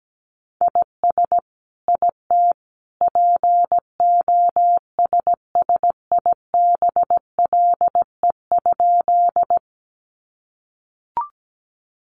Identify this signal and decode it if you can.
Morse code, heard as a keyed tone, about 17 wpm, 700 Hz. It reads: IS IT POSSIBLE?
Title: Sentences from Top 500 Words 17wpm